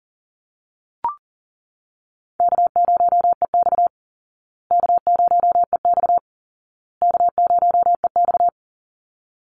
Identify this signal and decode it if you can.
Morse code, heard as a keyed tone, about 40 wpm, 700 Hz. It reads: K0EX K0EX K0EX
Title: Call Signs Repeated 2x - 40wpm